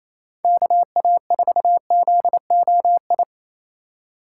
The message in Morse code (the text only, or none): KA4ZOI